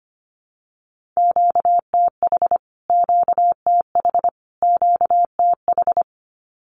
Morse code, heard as a keyed tone, about 25 wpm, 700 Hz. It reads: QTH QTH QTH
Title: Q-Codes Repeated 3x 25wpm